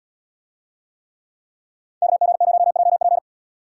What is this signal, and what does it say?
Morse code, heard as a keyed tone, about 78 wpm, 700 Hz. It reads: KG0JP